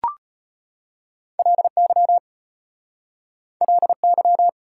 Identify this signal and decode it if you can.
Morse code, heard as a keyed tone, about 33 wpm, 690 Hz. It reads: LY LY